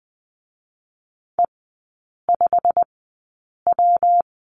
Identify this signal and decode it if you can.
Morse code, heard as a keyed tone, about 19 wpm, 700 Hz. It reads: E 5 W
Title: 121 Lesson 20e - Number 3 - Three Characters 20wpm